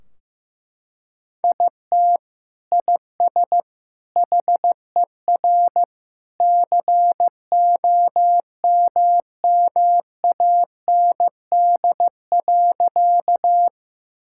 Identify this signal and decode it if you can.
Morse code, heard as a keyed tone, about 16 wpm, 700 Hz. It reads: IT IS HER COMMAND.
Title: Sentences from Top 1600 Words 15wpm